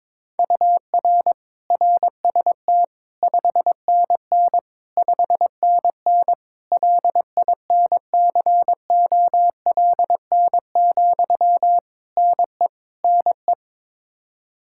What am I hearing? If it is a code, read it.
Morse code, heard as a keyed tone, about 22 wpm, 700 Hz. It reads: UR RST 5NN 5NN LINCOLN, NE NE
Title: Practice QSO - Exchange 1 - Beginning - 22wpm